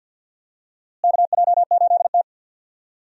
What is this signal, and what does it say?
Morse code, heard as a keyed tone, about 53 wpm, 700 Hz. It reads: KJ8T
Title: USA Call Signs - 4 Characters - Repeated 2x - 50wpm